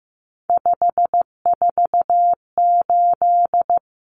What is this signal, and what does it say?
Morse code, heard as a keyed tone, about 15 wpm, 700 Hz. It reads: 548
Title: Numbers - 3-Digit 15wpm